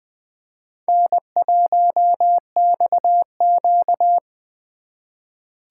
Morse code, heard as a keyed tone, about 20 wpm, 700 Hz. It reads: N1XQ
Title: USA Call Signs - 2x Character Spacing - 20wpm